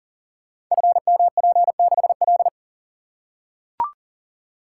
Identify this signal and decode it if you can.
Morse code, heard as a keyed tone, about 42 wpm, 700 Hz. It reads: FMP6L
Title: Code Groups - 5 Letter:Numbers - 40wpm